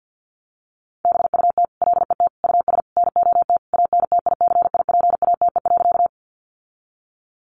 Morse code, heard as a keyed tone, about 50 wpm, 700 Hz. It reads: BUT LET US NOT ANTICIPATE.
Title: Sentences from Top 1500 Words 50wpm